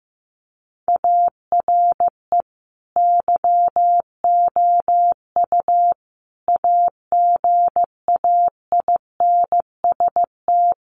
Morse code, heard as a keyed tone, about 15 wpm, 700 Hz. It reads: ARE YOU AGAINST
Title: Sentences from Top 1400 Words 15wpm